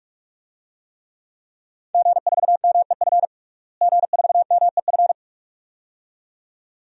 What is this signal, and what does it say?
Morse code, heard as a keyed tone, about 46 wpm, 700 Hz. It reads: G4MEF G4MEF
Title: DXCC-100 Call Signs - 5 Characters - Repeated 2x - 45wpm